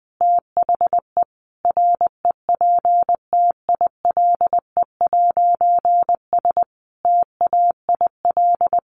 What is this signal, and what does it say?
Morse code, heard as a keyed tone, about 20 wpm, 700 Hz. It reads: THE REPTILE'S TAIL
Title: A Sound Of Thunder 20wpm